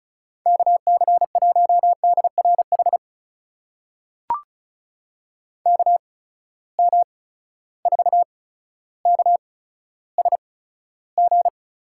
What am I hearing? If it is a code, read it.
Morse code, heard as a keyed tone, about 36 wpm, spaced as for 15 wpm, 700 Hz. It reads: KC1DRH KM4KSG
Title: USA Call Signs - 6 Characters - 8x Character Spacing - 35wpm